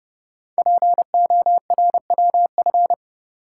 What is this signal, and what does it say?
Morse code, heard as a keyed tone, about 29 wpm, 700 Hz. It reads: PORWF
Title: Code Groups - 5 Letters - 30wpm